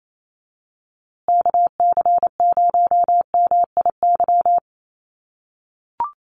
Morse code, heard as a keyed tone, about 28 wpm, 700 Hz. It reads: KC0MIY
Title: USA Call Signs Repeated 3x 28wpm